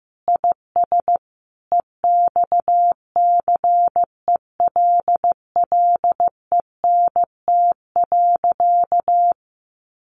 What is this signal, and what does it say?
Morse code, heard as a keyed tone, about 15 wpm, 700 Hz. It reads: IS EXCELLENT.
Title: Sentences from Top 1100 Words 15wpm